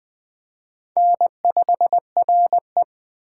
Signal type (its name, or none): Morse code